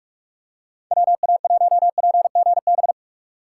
Morse code, heard as a keyed tone, about 41 wpm, 700 Hz. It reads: WA1PGB